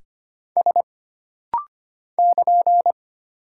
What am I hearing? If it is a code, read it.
Morse code, heard as a keyed tone, about 25 wpm, 700 Hz. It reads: S <KN>
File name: Single Character (Letter:Number:Punctuation:Prosign) - Rapid-Fire - 25wpm